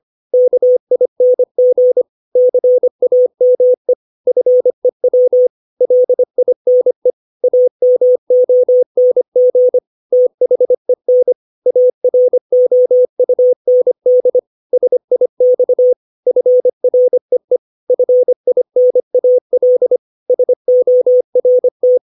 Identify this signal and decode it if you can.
Morse code, heard as a keyed tone, about 25 wpm, 500 Hz. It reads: KING CAME FEW LINE AMONG THEN AROUND SIX FREE FINAL SORT
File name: 632 words / 632 Words 25WPM